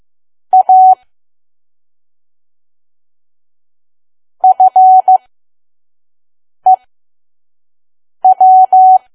Morse code, heard as a keyed tone, about 15 wpm, spaced as for 5 wpm, 750 Hz. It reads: A FEW
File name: Archive / 5 WPM files / PM